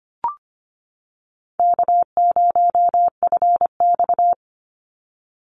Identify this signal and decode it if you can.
Morse code, heard as a keyed tone, about 25 wpm, 700 Hz. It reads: K0FX
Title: Call Signs - 25wpm